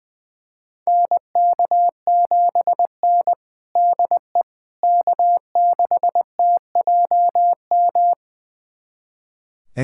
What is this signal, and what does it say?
Morse code, heard as a keyed tone, about 20 wpm, 700 Hz. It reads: NK7N DE K6TJM
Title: USA Callsign DE Callsign 20wpm